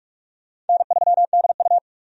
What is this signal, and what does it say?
Morse code, heard as a keyed tone, about 47 wpm, 700 Hz. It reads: N3DU
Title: USA Call Signs - 4 Characters - 45wpm